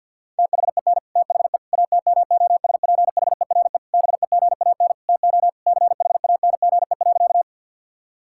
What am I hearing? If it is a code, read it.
Morse code, heard as a keyed tone, about 50 wpm, 700 Hz. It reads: THEN THE ATMOSPHERE BEGAN TO CHANGE.
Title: Sentences from Top 1600 Words 50wpm